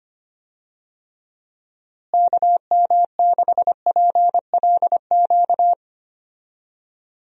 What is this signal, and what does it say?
Morse code, heard as a keyed tone, about 25 wpm, 700 Hz. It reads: KM6PLQ
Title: USA Call Signs - 6 Characters - Repeated 3x - 25wpm